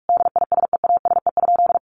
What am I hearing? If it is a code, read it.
Morse code, heard as a keyed tone, about 44 wpm, 700 Hz. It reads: DISEASE?
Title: Sentences from Top 1200 Words 45wpm